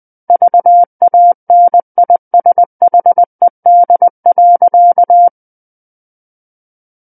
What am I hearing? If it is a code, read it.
Morse code, heard as a keyed tone, about 20 wpm, 700 Hz. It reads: VANISHED.
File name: 11-Chapter 11 - 20 WPM